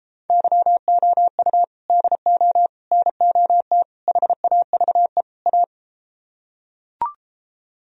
Morse code, heard as a keyed tone, about 33 wpm, 700 Hz. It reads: YOU DO NOT HAVE A